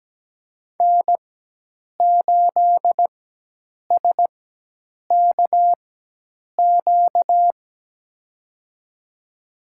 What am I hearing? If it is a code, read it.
Morse code, heard as a keyed tone, about 17 wpm, spaced as for 8 wpm, 700 Hz. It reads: N8SKQ
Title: USA Call Signs - 5 Characters - 4x Character Spacing - 17wpm